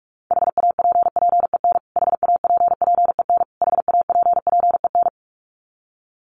Morse code, heard as a keyed tone, about 44 wpm, 700 Hz. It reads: HAPPEN HAPPEN HAPPEN
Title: Top 500 Words Repeated 45wpm